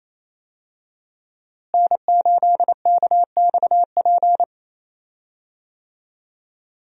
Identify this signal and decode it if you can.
Morse code, heard as a keyed tone, about 28 wpm, 700 Hz. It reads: N8KXP
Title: USA Call Signs - 5 Characters - Repeated 3x - 28wpm